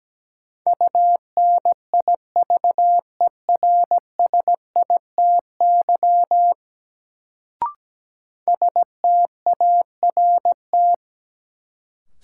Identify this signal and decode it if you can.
Morse code, heard as a keyed tone, about 17 wpm, 700 Hz. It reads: UNIVERSITY START
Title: Sets of 4 Words - New Word Drill - 17wpm